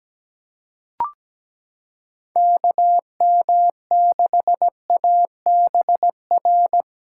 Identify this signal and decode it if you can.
Morse code, heard as a keyed tone, about 17 wpm, 700 Hz. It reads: KM6ABR
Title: USA Callsign DE Callsign 17wpm